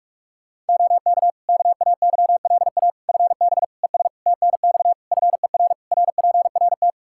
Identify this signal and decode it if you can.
Morse code, heard as a keyed tone, about 41 wpm, 700 Hz. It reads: OK KAYLA FB ES TNX FER RPRT
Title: K1USN SST QSO Practice 45wpm